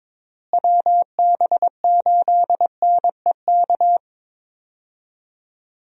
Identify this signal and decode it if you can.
Morse code, heard as a keyed tone, about 23 wpm, 700 Hz. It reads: WB8NEK
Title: USA Call Signs - 6 Characters - Repeated 2x - 22wpm